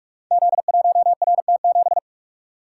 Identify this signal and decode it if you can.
Morse code, heard as a keyed tone, about 44 wpm, 700 Hz. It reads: Z1RT7
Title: Code Groups - 5 Letter:Numbers - 45wpm